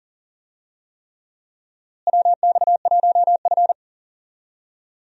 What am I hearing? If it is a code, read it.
Morse code, heard as a keyed tone, about 40 wpm, 700 Hz. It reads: WX1F
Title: USA Call Signs - 4 Characters - Repeated 3x - 40wpm